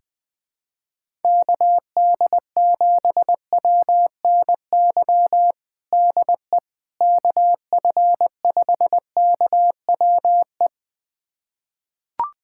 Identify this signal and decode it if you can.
Morse code, heard as a keyed tone, about 20 wpm, 700 Hz. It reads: KD7WNY DE KF5KWE
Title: USA Callsign DE Callsign - 6 Characters - 20wpm